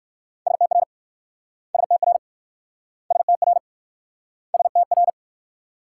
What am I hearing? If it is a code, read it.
Morse code, heard as a keyed tone, about 54 wpm, 690 Hz. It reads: STR STR STR STR
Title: Speed-Racing Multi-letter Phonemes - 45wpm